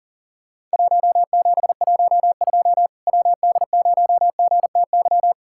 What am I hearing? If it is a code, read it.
Morse code, heard as a keyed tone, about 40 wpm, 700 Hz. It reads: 1712 WD0GTY
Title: Worked All Europe DX Contest - QTC Exchange - 40wpm